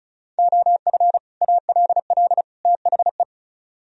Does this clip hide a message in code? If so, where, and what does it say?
Yes: Morse code from 0.4 to 3.2 s, heard as a keyed tone, about 46 wpm, 700 Hz. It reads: OF ALL THE